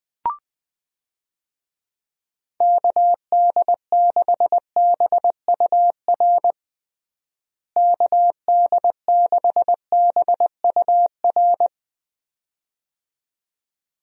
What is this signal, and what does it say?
Morse code, heard as a keyed tone, about 20 wpm, 700 Hz. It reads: KD6BUR KD6BUR
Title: USA Call Signs - 6 Characters - Repeated 2x - 20wpm